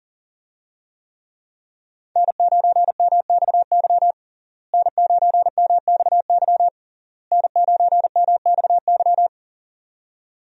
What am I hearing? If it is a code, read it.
Morse code, heard as a keyed tone, about 40 wpm, 700 Hz. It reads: N9MXY N9MXY N9MXY
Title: USA Call Signs - 5 Characters - Repeated 3x - 40wpm